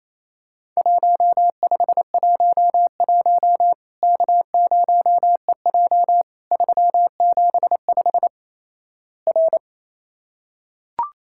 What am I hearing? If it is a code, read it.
Morse code, heard as a keyed tone, about 28 wpm, 700 Hz. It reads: 1511 K0EJ 375 R
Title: Worked All Europe DX Contest - QTC Exchange - 28wpm